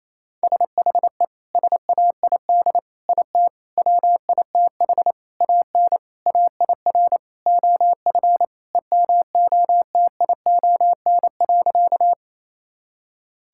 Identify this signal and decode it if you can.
Morse code, heard as a keyed tone, about 28 wpm, 700 Hz. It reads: SHE SAID IT WITH AN AIR OF EMOTION.
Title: Sentences from Top 1600 Words 28wpm